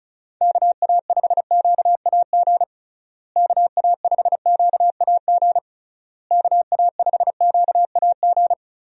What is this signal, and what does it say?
Morse code, heard as a keyed tone, about 36 wpm, 700 Hz. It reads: KA5QAG KA5QAG KA5QAG
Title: USA Call Signs - 6 Characters - Repeated 3x - 35wpm